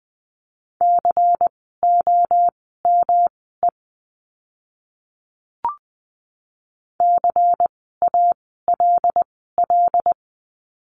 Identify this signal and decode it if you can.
Morse code, heard as a keyed tone, about 20 wpm, spaced as for 15 wpm, 700 Hz. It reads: COME CALL
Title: 115 Lesson 19f - Letter w - Words 20wpm